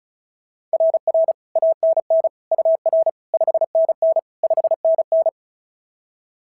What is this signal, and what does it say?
Morse code, heard as a keyed tone, about 36 wpm, 690 Hz. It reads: RR ANN UR 5NN 5NN
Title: SOTA QSO Practice 35wpm